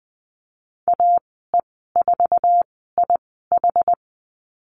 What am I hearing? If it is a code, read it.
Morse code, heard as a keyed tone, about 20 wpm, spaced as for 14 wpm, 700 Hz. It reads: AE4IH
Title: 062 Lesson 11g - Letter h - Call Signs 20wpm